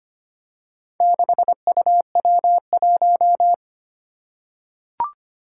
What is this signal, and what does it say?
Morse code, heard as a keyed tone, about 25 wpm, 700 Hz. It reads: 6UW1
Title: Four Challenging Letter-Number Combinations 25wpm